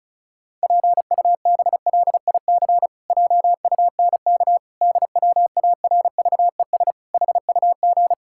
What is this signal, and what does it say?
Morse code, heard as a keyed tone, about 36 wpm, 700 Hz. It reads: PUBLIC JUNK DWARVES HUG
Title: Pangram Challenge v2 - 35wpm